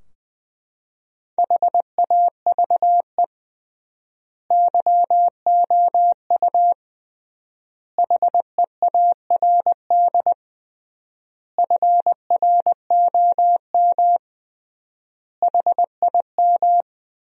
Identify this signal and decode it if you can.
Morse code, heard as a keyed tone, about 20 wpm, 700 Hz. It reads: HAVE YOU HEARD FROM HIM
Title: 3x Word-Spacing - Sentences from Top 400 Words - 20wpm